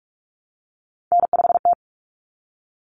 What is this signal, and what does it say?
Morse code, heard as a keyed tone, about 44 wpm, 700 Hz. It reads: N5T
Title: Common Cut Numbers - 3-Digit 45wpm